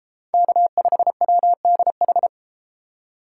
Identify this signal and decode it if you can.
Morse code, heard as a keyed tone, about 32 wpm, 700 Hz. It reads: K5WDH